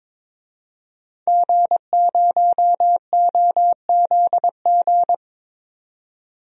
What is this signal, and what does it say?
Morse code, heard as a keyed tone, about 22 wpm, 700 Hz. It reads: G0OZG
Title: DXCC-100 Call Signs - 5 Characters - 22wpm